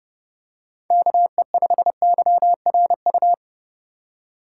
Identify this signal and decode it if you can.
Morse code, heard as a keyed tone, about 30 wpm, 700 Hz. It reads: KE5YRU
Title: USA Call Signs - 6 Characters - 3x Character Spacing - 30wpm